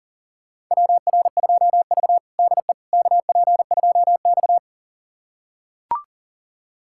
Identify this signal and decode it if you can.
Morse code, heard as a keyed tone, about 40 wpm, 700 Hz. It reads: WR2V DE KP2X
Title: USA Callsign DE Callsign - 4 Characters - 40wpm